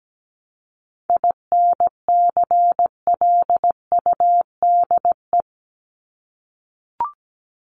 Morse code, heard as a keyed tone, about 17 wpm, 700 Hz. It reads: INCLUDE
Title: ED Suffix - Top 100 Words with Base Words 17wpm